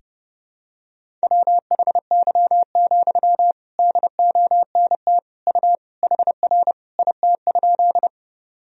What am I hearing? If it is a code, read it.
Morse code, heard as a keyed tone, about 32 wpm, 700 Hz. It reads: WHY, DONT U HR IT?
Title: Sentences from Top 300 Words and Abbreviations 30wpm